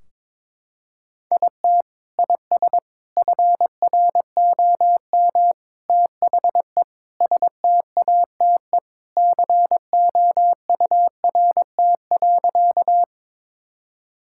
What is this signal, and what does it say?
Morse code, heard as a keyed tone, about 21 wpm, 700 Hz. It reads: IT IS FROM THE STATE COURT.
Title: Sentences from Top 1600 Words 22wpm